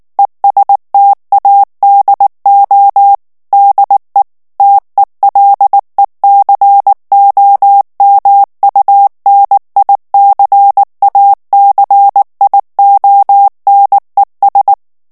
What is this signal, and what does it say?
Morse code, heard as a keyed tone, about 19 wpm, 800 Hz. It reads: ESTADO DE TELECOMUNICACIONES
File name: morse mp3 / ESPAGNOL TEXTE EN CLAIR 2 VITESSE 18 MOTS MINUTES